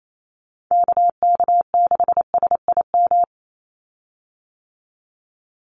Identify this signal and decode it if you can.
Morse code, heard as a keyed tone, about 28 wpm, 700 Hz. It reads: KK6SIM
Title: USA Call Signs 28wpm